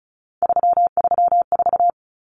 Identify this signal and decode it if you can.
Morse code, heard as a keyed tone, about 34 wpm, 700 Hz. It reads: 334
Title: Numbers - 3-Digit 35wpm